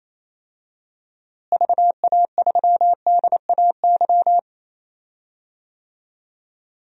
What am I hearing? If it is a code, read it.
Morse code, heard as a keyed tone, about 28 wpm, 700 Hz. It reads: VA3DAY
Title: DXCC-100 Call Signs - 6 Characters - Repeated 3x - 28wpm